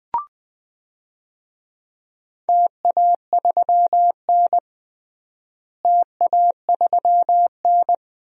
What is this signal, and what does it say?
Morse code, heard as a keyed tone, about 20 wpm, 700 Hz. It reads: TA3N TA3N
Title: DXCC-100 Call Signs - 4 Characters - Repeated 2x - 20wpm